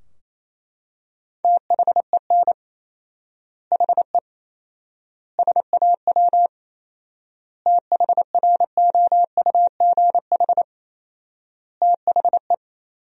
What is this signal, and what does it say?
Morse code, heard as a keyed tone, about 27 wpm, 700 Hz. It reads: THEN HE SAW THROUGH THE
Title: 4x Word-Spacing - Sentences from Top 400 Words - 28wpm